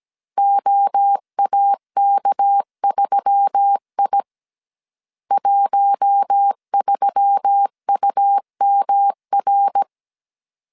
Morse code, heard as a keyed tone, about 17 wpm, 800 Hz. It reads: OAK3I 13UMR
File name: 16wpm